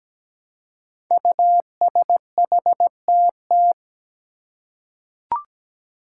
Morse code, heard as a keyed tone, about 17 wpm, 700 Hz. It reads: USHTT
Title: Five Challenging Letter-Number Combinations 17wpm